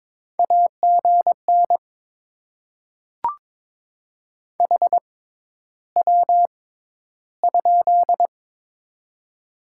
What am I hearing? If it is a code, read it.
Morse code, heard as a keyed tone, about 22 wpm, spaced as for 11 wpm, 700 Hz. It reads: AGN HW?
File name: Top 20 QSO Elements - 6x Character Spacing - 22wpm